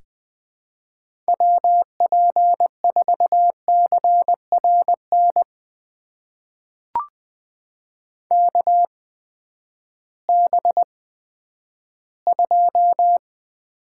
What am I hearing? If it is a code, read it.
Morse code, heard as a keyed tone, about 20 wpm, spaced as for 11 wpm, 700 Hz. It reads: WP4CRN KB2
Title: USA Call Signs - 6 Characters - 8x Character Spacing - 20wpm